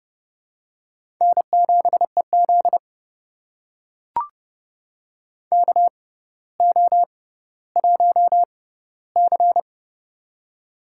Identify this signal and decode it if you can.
Morse code, heard as a keyed tone, about 30 wpm, spaced as for 15 wpm, 700 Hz. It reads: N7EZ KO1C
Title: USA Call Signs - 4 Characters - 6x Character Spacing - 30wpm